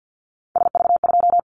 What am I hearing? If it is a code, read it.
Morse code, heard as a keyed tone, about 50 wpm, 700 Hz. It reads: SV2
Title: Callsign Prefixes for Top 50 DXCC - 50wpm